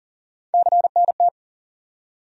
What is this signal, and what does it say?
Morse code, heard as a keyed tone, about 40 wpm, 700 Hz. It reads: CNT
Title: USA Call Sign Postfixes - 3 Letters 40wpm